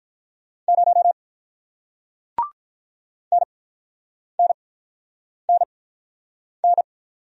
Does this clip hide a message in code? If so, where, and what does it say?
Yes: Morse code from 0.7 to 6.8 s, heard as a keyed tone, about 48 wpm, spaced as for 9 wpm, 700 Hz. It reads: 0 NNNN